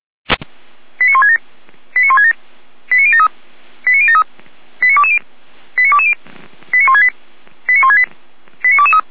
Five-tone selective call (ZVEI), heard as quick runs of five tones, tones of about 70 ms. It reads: 89167 89167 89073 89073 89220 89220 89168 89168 89223